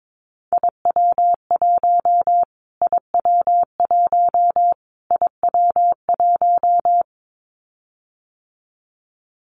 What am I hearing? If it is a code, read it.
Morse code, heard as a keyed tone, about 22 wpm, 700 Hz. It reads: IW1 IW1 IW1
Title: Callsign Prefixes for Top 25 DXCC - Repeated 3x - 22wpm